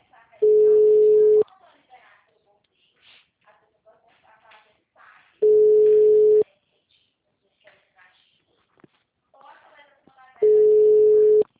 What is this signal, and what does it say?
Ringback tone (Europe, 425 Hz), with speech.